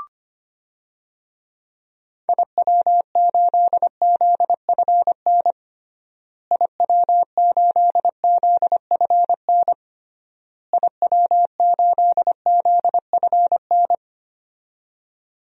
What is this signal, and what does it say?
Morse code, heard as a keyed tone, about 25 wpm, 700 Hz. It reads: IW8ZFN IW8ZFN IW8ZFN
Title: DXCC-100 Call Signs - 6 Characters - Repeated 3x - 25wpm